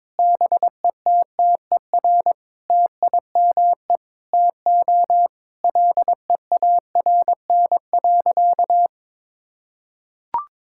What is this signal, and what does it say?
Morse code, heard as a keyed tone, about 22 wpm, 700 Hz. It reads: BETTER TIME TO LEARN.
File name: practice / 22wpm / Sentences with 8-Word Phrases - 22wpm